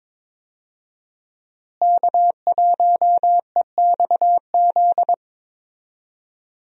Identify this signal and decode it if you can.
Morse code, heard as a keyed tone, about 22 wpm, 700 Hz. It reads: K1EXZ
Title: USA Call Signs - 5 Characters - Repeated 3x - 22wpm